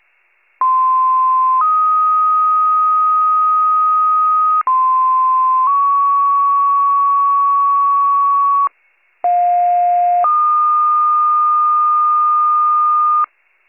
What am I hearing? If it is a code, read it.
Two-tone page, heard as a one-second tone, then a three-second tone, about 1040 then 1250 Hz, about 1040 then 1120 Hz, about 710 then 1220 Hz. It is heard from 0.6 to 13.3 s.